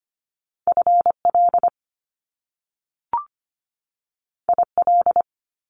Morse code, heard as a keyed tone, about 25 wpm, 700 Hz. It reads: FL IL
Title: Top Two Letter Combinations in Words 25wpm